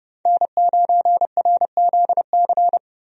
Morse code, heard as a keyed tone, about 31 wpm, 700 Hz. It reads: N9RZC
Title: USA Call Signs - 5 Characters - 30wpm